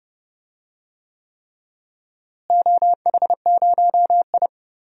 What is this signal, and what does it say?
Morse code, heard as a keyed tone, about 30 wpm, 700 Hz. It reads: OH0I
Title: DXCC-100 Call Signs - 4 Characters - Repeated 3x - 30wpm